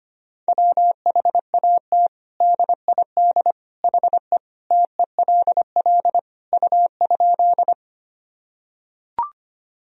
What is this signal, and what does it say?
Morse code, heard as a keyed tone, about 25 wpm, 700 Hz. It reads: WHAT DID HE TELL U?
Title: Sentences from Top 200 Words and Abbreviations 25wpm